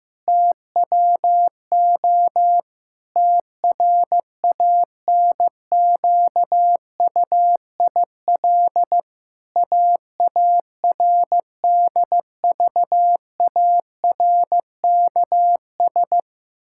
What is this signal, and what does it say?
Morse code, heard as a keyed tone, about 15 wpm, 700 Hz. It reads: TWO TRANQUIL AARDVARKS
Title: Pangram Challenge v2 - 15wpm